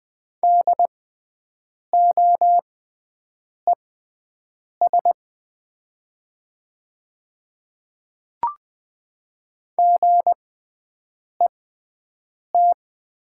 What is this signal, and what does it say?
Morse code, heard as a keyed tone, about 20 wpm, spaced as for 7 wpm, 700 Hz. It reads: DOES GET
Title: 6x Character Spacing - Top 200 Words - 20wpm